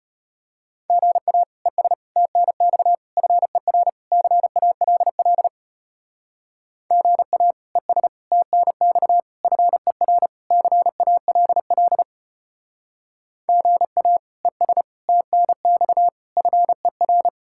Morse code, heard as a keyed tone, about 31 wpm, 700 Hz. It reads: GA ES TNX FER CALL GA ES TNX FER CALL GA ES TNX FER
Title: Speed-Racing Common QSO Phrases - 25wpm